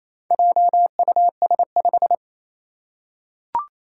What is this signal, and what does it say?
Morse code, heard as a keyed tone, about 27 wpm, 700 Hz. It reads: JUS5
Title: Four Challenging Letter-Number Combinations 28wpm